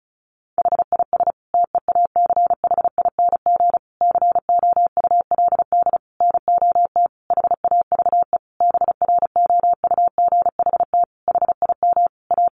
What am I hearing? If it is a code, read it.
Morse code, heard as a keyed tone, about 34 wpm, 700 Hz. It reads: HIS TEACHING COULD NOT HAVE BROUGHT HIM A
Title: Sentences from Top 1100 Words 35wpm